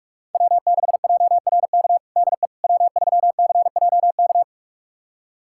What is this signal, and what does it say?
Morse code, heard as a keyed tone, about 41 wpm, 700 Hz. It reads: W6JRK DE W3CJK
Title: USA Callsign DE Callsign - 5 Characters - 45wpm